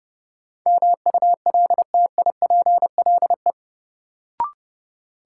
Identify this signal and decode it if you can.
Morse code, heard as a keyed tone, about 30 wpm, 700 Hz. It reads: MULTIPLE
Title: Sets of 5 Words - New Word Drill - 30wpm